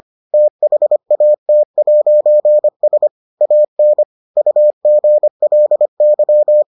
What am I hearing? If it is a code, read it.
Morse code, heard as a keyed tone, about 25 wpm, 600 Hz. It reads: THAT'S AN UGLY